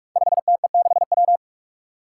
Morse code, heard as a keyed tone, about 44 wpm, 700 Hz. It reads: 5TE6W
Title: Five Challenging Letter-Number Combinations 45wpm